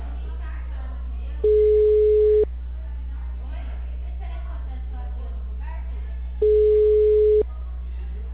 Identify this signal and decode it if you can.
Ringback tone (Europe, 425 Hz), with speech.